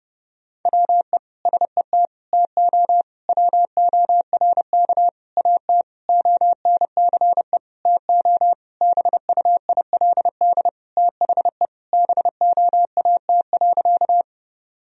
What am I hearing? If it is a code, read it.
Morse code, heard as a keyed tone, about 30 wpm, 700 Hz. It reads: WE SET TO WORK AT ONCE TO BUILD THE BOAT.
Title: Sentences from Top 1000 Words - Review Part 1 - 30wpm